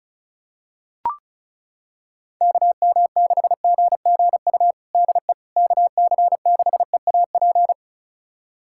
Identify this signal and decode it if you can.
Morse code, heard as a keyed tone, about 34 wpm, 700 Hz. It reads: KM6GGU DE KC6EAP
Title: USA Callsign DE Callsign 35wpm